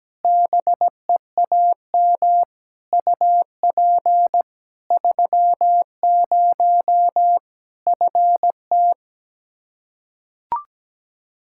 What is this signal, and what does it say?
Morse code, heard as a keyed tone, about 17 wpm, 700 Hz. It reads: BEAM UP 30 FT
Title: Practice QSO - Exchange 2 - Info - 17wpm